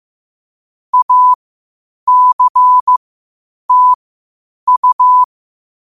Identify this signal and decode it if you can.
Morse code, heard as a keyed tone, about 15 wpm, spaced as for 8 wpm, 1000 Hz. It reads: ACTU
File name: qotdfiles / 8wpm